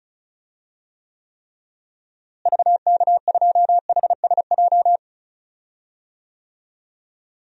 Morse code, heard as a keyed tone, about 36 wpm, 700 Hz. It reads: VK2HSJ
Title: DXCC-100 Call Signs - 6 Characters - Repeated 3x - 35wpm